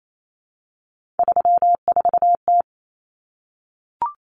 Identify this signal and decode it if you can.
Morse code, heard as a keyed tone, about 28 wpm, 700 Hz. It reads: 34T
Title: Common Cut Numbers - 3-Digit 28wpm